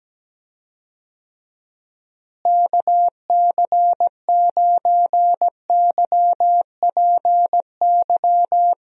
Morse code, heard as a keyed tone, about 17 wpm, 700 Hz. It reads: KC9YPY
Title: USA Call Signs - 6 Characters - Repeated 2x - 17wpm